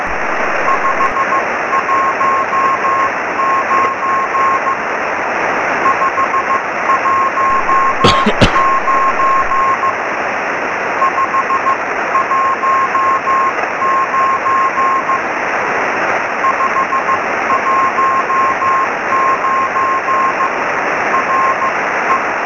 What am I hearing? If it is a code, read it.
Morse code, heard as a keyed tone, about 15 wpm, spaced as for 13 wpm, 1060 Hz, with radio noise. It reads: 519 519 519 519 RE
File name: sm7lcb / gm3sbc 23cm